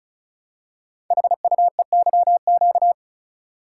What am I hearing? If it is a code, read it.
Morse code, heard as a keyed tone, about 36 wpm, 700 Hz. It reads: HUEYQ
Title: Code Groups - 5 Letters - 35wpm